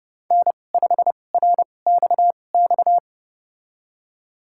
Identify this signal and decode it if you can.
Morse code, heard as a keyed tone, about 30 wpm, spaced as for 22 wpm, 700 Hz. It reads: N5RXX
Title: USA Call Signs - 5 Characters - 2x Character Spacing - 30wpm